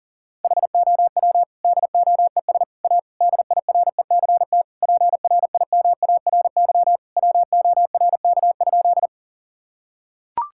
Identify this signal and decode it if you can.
Morse code, heard as a keyed tone, about 37 wpm, 700 Hz. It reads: HOW DOES A DIRECT PRIMARY WORK?
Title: Sentences from Top 1000 Words - Review Part 5 - 40wpm